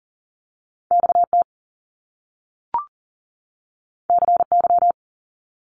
Morse code, heard as a keyed tone, about 41 wpm, 700 Hz. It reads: XT CY
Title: Top Two Letter Combinations in Words 40wpm